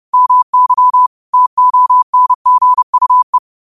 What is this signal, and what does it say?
Morse code, heard as a keyed tone, about 30 wpm, 1000 Hz. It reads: MY TONGUE
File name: qotdfiles / 30wpm